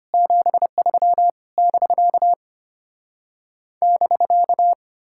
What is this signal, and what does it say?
Morse code, heard as a keyed tone, about 33 wpm, 700 Hz. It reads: 73 <BK> <BK>